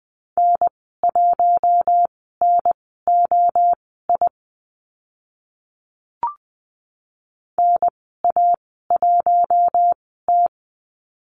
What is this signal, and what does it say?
Morse code, heard as a keyed tone, about 20 wpm, spaced as for 14 wpm, 700 Hz. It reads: N1NOI NA1T
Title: 042 Lesson 8g - Number 1 - Call Signs 20wpm